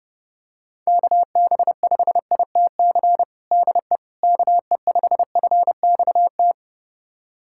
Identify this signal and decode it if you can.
Morse code, heard as a keyed tone, about 32 wpm, 700 Hz. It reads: KB5ITC DE KE5FXT
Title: USA Callsign DE Callsign - 6 Characters - 30wpm